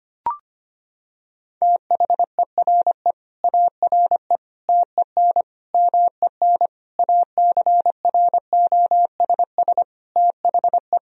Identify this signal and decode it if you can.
Morse code, heard as a keyed tone, about 25 wpm, 700 Hz. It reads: THERE ARE TEN MEN ACROSS THE
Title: Sentences from Top 1000 Words - Review Part 2 - 25wpm